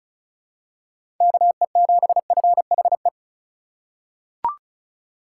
Morse code, heard as a keyed tone, about 34 wpm, 700 Hz. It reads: KE7FHE
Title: USA Call Signs - 4x Character Spacing - 35wpm